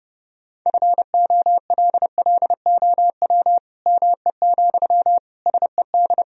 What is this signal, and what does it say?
Morse code, heard as a keyed tone, about 30 wpm, 700 Hz. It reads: FOLLOW ME, SED
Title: Sentences from Top 200 Words and Abbreviations 30wpm